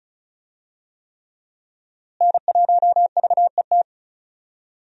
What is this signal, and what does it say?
Morse code, heard as a keyed tone, about 36 wpm, 700 Hz. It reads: N1VET
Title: USA Call Signs - 5 Characters - Repeated 3x - 35wpm